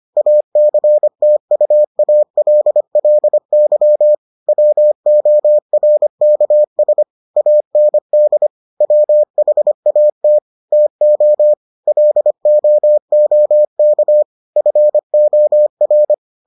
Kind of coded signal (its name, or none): Morse code